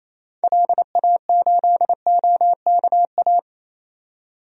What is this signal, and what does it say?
Morse code, heard as a keyed tone, about 27 wpm, 700 Hz. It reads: LA8OKA
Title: DXCC-100 Call Signs - 6 Characters - 28wpm